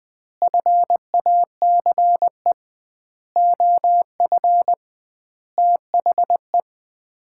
Morse code, heard as a keyed tone, about 21 wpm, 700 Hz. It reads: FACE OF THE
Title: 2x Word-Spacing - Sets of 4 Words - 20wpm